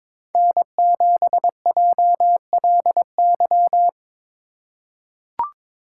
Morse code, heard as a keyed tone, about 22 wpm, 700 Hz. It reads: N7JLY
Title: USA Call Signs - 2x Character Spacing - 22wpm